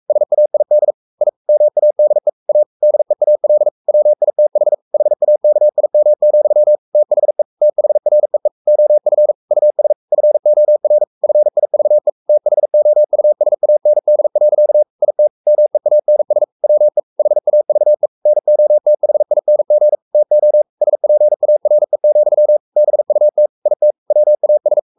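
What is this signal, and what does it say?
Morse code, heard as a keyed tone, about 44 wpm, 600 Hz. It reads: SAID I MADE A DEAL WITH HAKIM, THE THREE OF US FOR FIVE THOUSAND. IT MEANS WE HAVE NOTHING TO SPARE, BUT IT WAS
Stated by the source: Morse code at 43 wpm sending Ken Follett's novel Never in English, from CW med Gnister.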